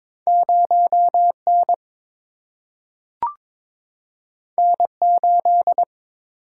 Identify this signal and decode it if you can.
Morse code, heard as a keyed tone, about 22 wpm, 700 Hz. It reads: 0 N N 8
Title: Single-Letter Single-Number 22wpm